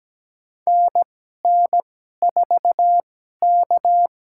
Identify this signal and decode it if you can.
Morse code, heard as a keyed tone, about 17 wpm, spaced as for 12 wpm, 700 Hz. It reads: NN4K
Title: USA Call Signs - 4 Characters - 2x Character Spacing - 17wpm